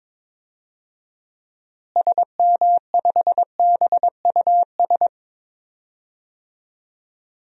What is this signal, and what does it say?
Morse code, heard as a keyed tone, about 22 wpm, 700 Hz. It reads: SM5BUS
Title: DXCC-100 Call Signs - 6 Characters - Repeated 2x - 22wpm